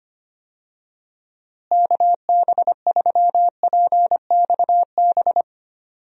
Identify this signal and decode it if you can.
Morse code, heard as a keyed tone, about 25 wpm, 700 Hz. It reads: KB3PXB
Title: USA Call Signs - 6 Characters - Repeated 2x - 25wpm